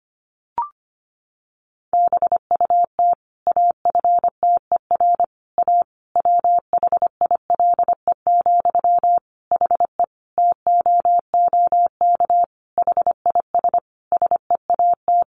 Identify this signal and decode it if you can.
Morse code, heard as a keyed tone, about 25 wpm, 700 Hz. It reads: BUT AFTER A WHILE, HE TOOK HIS SEAT
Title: Sentences from Top 1500 Words 25wpm